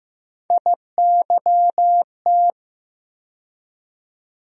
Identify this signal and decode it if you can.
Morse code, heard as a keyed tone, about 16 wpm, 700 Hz. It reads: IYT
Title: USA Call Sign Postfixes - 3 Letters 15wpm